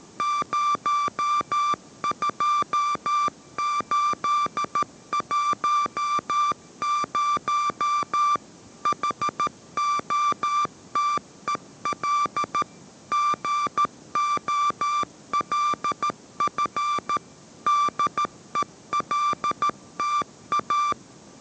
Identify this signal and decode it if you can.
Morse code, heard as a keyed tone, about 13 wpm, 1180 Hz. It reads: 02810 HOTEL GOLF DELTA